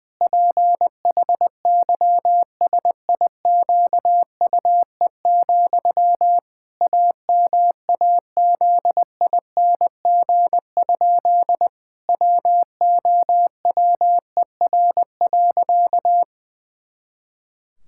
Morse code, heard as a keyed tone, about 20 wpm, 700 Hz. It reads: PHYSIQUE, AMAZING? WOWER.
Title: Pangram Challenge v2 - 20wpm